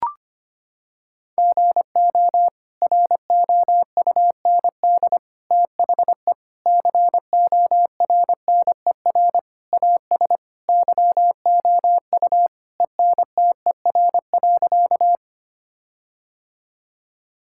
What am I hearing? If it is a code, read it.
Morse code, heard as a keyed tone, about 25 wpm, 700 Hz. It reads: GO ROUND THE CORNER AS YOU ENTER.
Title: Sentences from Top 1600 Words 25wpm